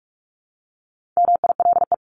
Morse code, heard as a keyed tone, about 44 wpm, 700 Hz. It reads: MILE
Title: ES:S Suffix - Top 100 Words with Base Words 45wpm